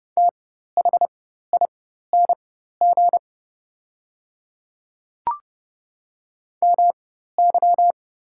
Morse code, heard as a keyed tone, about 30 wpm, spaced as for 14 wpm, 700 Hz. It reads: THING MY
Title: 4x Character Spacing - Top 100 Words - 30wpm